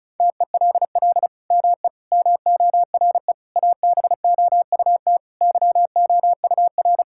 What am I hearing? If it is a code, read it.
Morse code, heard as a keyed tone, about 34 wpm, 700 Hz. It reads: TELL ME MORE ABOUT YOUR
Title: Sentences from Top 1000 Words - Review Part 6 - 35wpm